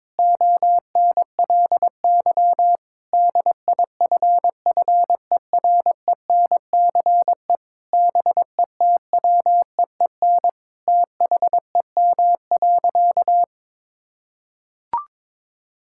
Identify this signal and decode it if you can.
Morse code, heard as a keyed tone, about 22 wpm, 700 Hz. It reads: ONLY DIFFERENCE BETWEEN THEM.
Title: Sentences from Top 1000 Words - Review Part 5 - 22wpm